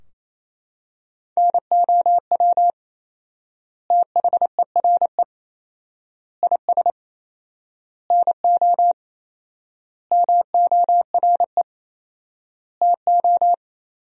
Morse code, heard as a keyed tone, about 28 wpm, 700 Hz. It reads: NOW THERE IS NO MORE TO
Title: 4x Word-Spacing - Sentences from Top 100 Words - 28wpm